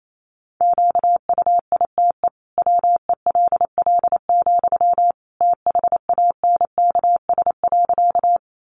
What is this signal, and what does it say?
Morse code, heard as a keyed tone, about 28 wpm, 700 Hz. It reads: QUITE WELL, THANKS.
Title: Sentences from Top 700 Words 28wpm